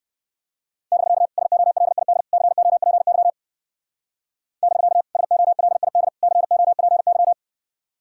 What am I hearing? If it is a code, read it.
Morse code, heard as a keyed tone, about 56 wpm, 700 Hz. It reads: <BK> SOLID COPY <BK> SOLID COPY
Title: Speed-Racing Common QSO Phrases Encore - 45wpm